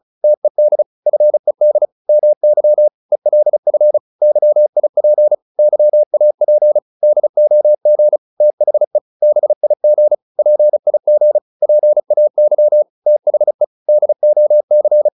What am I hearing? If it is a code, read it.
Morse code, heard as a keyed tone, about 36 wpm, 600 Hz. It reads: TED FED MY ELF YIP YAP DOG THE BIG PIG PAY THE DOC